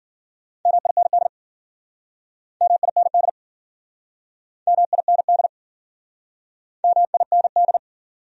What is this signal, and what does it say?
Morse code, heard as a keyed tone, about 57 wpm, 700 Hz. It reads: MIND MIND MIND MIND